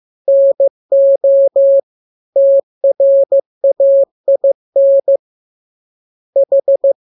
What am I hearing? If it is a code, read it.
Morse code, heard as a keyed tone, about 15 wpm, 550 Hz. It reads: NO TRAIN H
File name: jokes / joke 11 15 0000